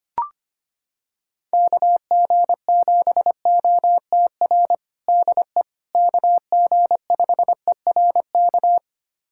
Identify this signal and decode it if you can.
Morse code, heard as a keyed tone, about 25 wpm, 700 Hz. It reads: KG7OTR DE KG5ERK
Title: USA Callsign DE Callsign - 6 Characters - 25wpm